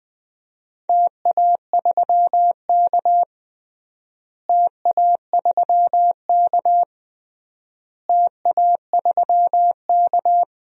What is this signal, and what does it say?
Morse code, heard as a keyed tone, about 20 wpm, 700 Hz. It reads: TA3K TA3K TA3K
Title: DXCC-100 Call Signs - 4 Characters - Repeated 3x - 20wpm